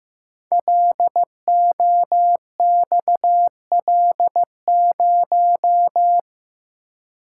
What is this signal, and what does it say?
Morse code, heard as a keyed tone, about 15 wpm, 700 Hz. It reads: LOXL0
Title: Code Groups - 5 Letter:Numbers - 15wpm